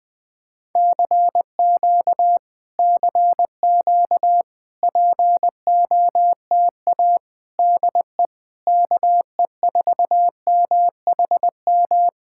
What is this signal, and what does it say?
Morse code, heard as a keyed tone, about 19 wpm, 700 Hz. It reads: CQ CQ POTA DE KE4MHM
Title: Speed-Racing Common QSO Phrases Encore - 20wpm